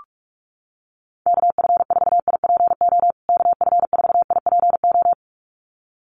Morse code, heard as a keyed tone, about 44 wpm, 700 Hz. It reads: KF4IPO KF4IPO
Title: USA Call Signs Repeated 2x 45wpm